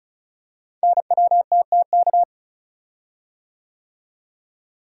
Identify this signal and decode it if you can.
Morse code, heard as a keyed tone, about 36 wpm, 690 Hz. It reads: NWTTK
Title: Code Groups - 5 Letters - 35wpm